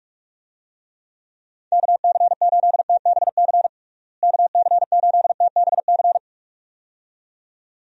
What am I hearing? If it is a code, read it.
Morse code, heard as a keyed tone, about 46 wpm, 700 Hz. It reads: KC8TBC KC8TBC
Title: USA Call Signs - 6 Characters - Repeated 2x - 45wpm